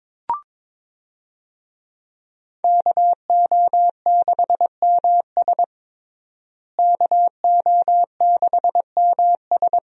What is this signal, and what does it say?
Morse code, heard as a keyed tone, about 21 wpm, 700 Hz. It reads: KO6MS KO6MS
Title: USA Call Signs - 5 Characters - Repeated 2x - 22wpm